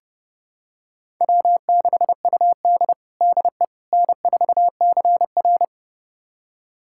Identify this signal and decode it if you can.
Morse code, heard as a keyed tone, about 31 wpm, 700 Hz. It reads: W6UD DE N4CR
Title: USA Callsign DE Callsign - 4 Characters - 30wpm